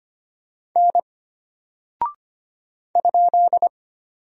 Morse code, heard as a keyed tone, about 25 wpm, 700 Hz. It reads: N ?
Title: Single Character (Letter:Number:Punctuation:Prosign) Alt 25wpm